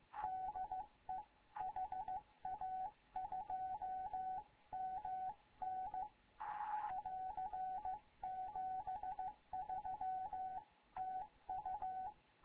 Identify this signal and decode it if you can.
Morse code, heard as a keyed tone, about 15 wpm, 750 Hz, with radio noise. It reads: DE HA2MN <AR> 73 TU
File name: Morze-HA2MN / hangp / 2 11 GYAKORLAT